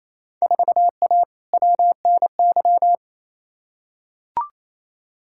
Morse code, heard as a keyed tone, about 28 wpm, 700 Hz. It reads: 4A WNY
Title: Field Day Exchanges Encore - 28wpm